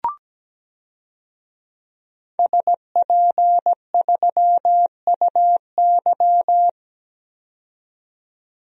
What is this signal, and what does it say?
Morse code, heard as a keyed tone, about 17 wpm, 700 Hz. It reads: SP3UY
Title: DXCC-100 Call Signs - 5 Characters - 17wpm